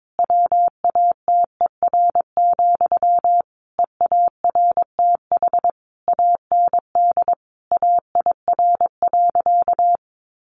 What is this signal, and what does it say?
Morse code, heard as a keyed tone, about 22 wpm, 700 Hz. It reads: WATER, EARTH AND AIR.
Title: Sentences from Top 400 Words 22wpm